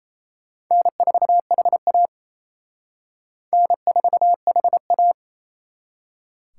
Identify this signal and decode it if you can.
Morse code, heard as a keyed tone, about 34 wpm, 700 Hz. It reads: N4HA N4HA
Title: Speed Racing - USA Call Signs 28wpm